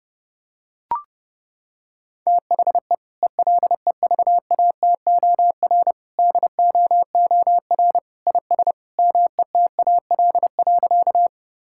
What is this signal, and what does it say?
Morse code, heard as a keyed tone, about 30 wpm, 700 Hz. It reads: THE ELEVATOR DOOR IS METAL.
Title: Sentences from Top 1600 Words 30wpm